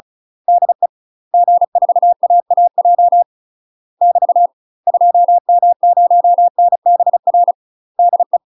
Morse code, heard as a keyed tone, about 34 wpm, 700 Hz. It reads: DE G4AAJ <BT> 2M0NBR DE
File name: UK MASTER QSOs TRAINING / 35WPM / UK MASTER QSOs 0030